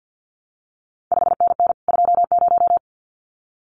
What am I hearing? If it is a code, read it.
Morse code, heard as a keyed tone, about 50 wpm, 700 Hz. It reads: 5NN 20
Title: CQ WW DX Contest Exchange - 50wpm